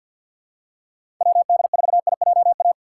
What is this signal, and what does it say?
Morse code, heard as a keyed tone, about 47 wpm, 700 Hz. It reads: WD4IJA
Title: USA Call Signs - 6 Characters - 3x Character Spacing - 50wpm